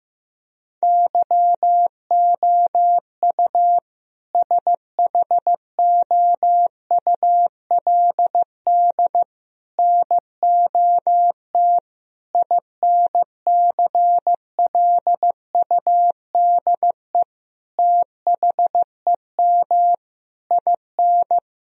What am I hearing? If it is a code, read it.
Morse code, heard as a keyed tone, about 15 wpm, 700 Hz. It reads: YOU SHOULD NOT INCLUDE THEM IN
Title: Sentences from Top 1000 Words - Review Part 3 - 15wpm